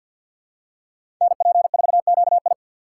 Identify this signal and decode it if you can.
Morse code, heard as a keyed tone, about 52 wpm, 700 Hz. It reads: NP4QI
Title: USA Call Signs - 5 Characters - 6x Character Spacing - 50wpm